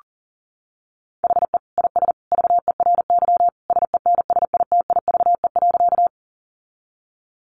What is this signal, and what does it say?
Morse code, heard as a keyed tone, about 40 wpm, 700 Hz. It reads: HE IS VERY SENSITIVE.
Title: Sentences from Top 1400 Words 40wpm